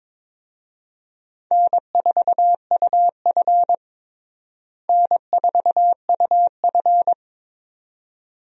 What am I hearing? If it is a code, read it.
Morse code, heard as a keyed tone, about 22 wpm, 700 Hz. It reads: N4UF N4UF
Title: USA Call Signs - 4 Characters - Repeated 2x - 22wpm